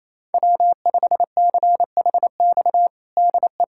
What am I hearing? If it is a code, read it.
Morse code, heard as a keyed tone, about 28 wpm, 700 Hz. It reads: W5CHX DE
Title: USA Callsign DE Callsign 28wpm